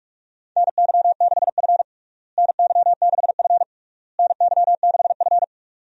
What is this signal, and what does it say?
Morse code, heard as a keyed tone, about 46 wpm, 700 Hz. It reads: NY6F NY6F NY6F
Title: USA Call Signs - 4 Characters - Repeated 3x - 45wpm